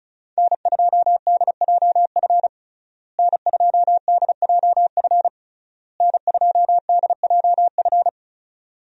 Morse code, heard as a keyed tone, about 34 wpm, 700 Hz. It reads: N2DJF N2DJF N2DJF
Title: USA Call Signs - 5 Characters - Repeated 3x - 35wpm